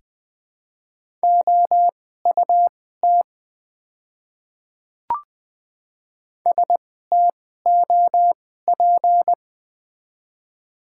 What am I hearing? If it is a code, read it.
Morse code, heard as a keyed tone, about 20 wpm, spaced as for 14 wpm, 700 Hz. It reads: OUT STOP
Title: 2x Character Spacing - Top 500 Words - 20wpm